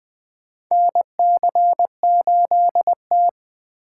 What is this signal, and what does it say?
Morse code, heard as a keyed tone, about 20 wpm, 700 Hz. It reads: NC8T
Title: USA Call Signs - 4 Characters - 2x Character Spacing - 20wpm